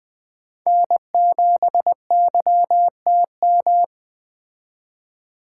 Morse code, heard as a keyed tone, about 21 wpm, 700 Hz. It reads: N7YTM